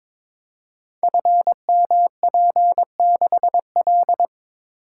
Morse code, heard as a keyed tone, about 22 wpm, 700 Hz. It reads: FMP6L
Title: Code Groups - 5 Letter:Numbers - 22wpm